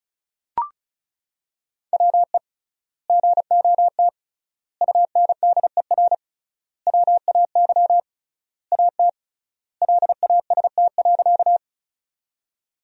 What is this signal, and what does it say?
Morse code, heard as a keyed tone, about 36 wpm, 700 Hz. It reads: WE GOT UNDER WAY AT LAST.
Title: 3x Word-Spacing - Sentences from Top 300 Words - 35wpm